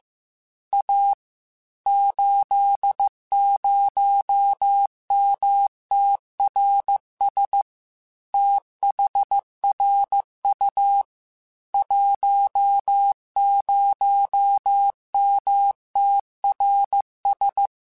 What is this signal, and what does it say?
Morse code, heard as a keyed tone, about 15 wpm, 800 Hz. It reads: A 80MTRS THRU 10MTRS
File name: QSOs / QSOs-mp3s / WD0FBT 1@13